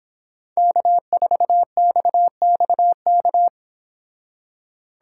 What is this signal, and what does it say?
Morse code, heard as a keyed tone, about 26 wpm, 700 Hz. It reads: K4XXK